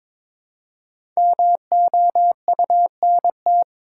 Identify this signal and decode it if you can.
Morse code, heard as a keyed tone, about 22 wpm, 700 Hz. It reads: MOUNT
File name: 5 Letter Words - First 3 Letters Given - 22wpm